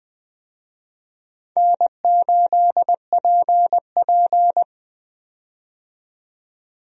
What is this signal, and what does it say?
Morse code, heard as a keyed tone, about 20 wpm, 700 Hz. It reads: N8PP
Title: USA Call Signs - 4 Characters - 20wpm